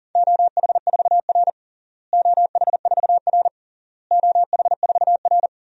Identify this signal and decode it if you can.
Morse code, heard as a keyed tone, about 40 wpm, 700 Hz. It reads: OH4R OH4R OH4R
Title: DXCC-100 Call Signs - 4 Characters - Repeated 3x - 40wpm